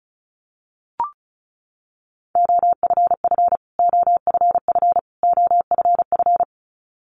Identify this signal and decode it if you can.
Morse code, heard as a keyed tone, about 34 wpm, 700 Hz. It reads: OFF OFF OFF
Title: Top 200 Words Repeated 35wpm